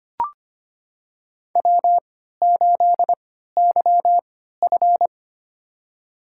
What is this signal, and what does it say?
Morse code, heard as a keyed tone, about 25 wpm, spaced as for 14 wpm, 700 Hz. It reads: W8YF
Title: USA Call Signs - 4 Characters - 3x Character Spacing - 25wpm